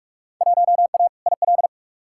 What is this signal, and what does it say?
Morse code, heard as a keyed tone, about 41 wpm, 700 Hz. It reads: 1A IL
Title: Field Day Exchanges Encore - 45wpm